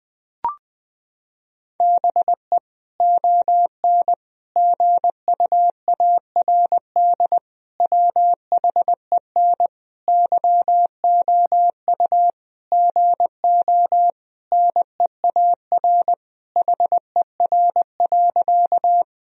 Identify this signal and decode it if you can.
Morse code, heard as a keyed tone, about 20 wpm, 700 Hz. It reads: BE ON GUARD WHEN YOU GO NEAR HER.
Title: Sentences from Top 1000 Words - Review Part 4 - 20wpm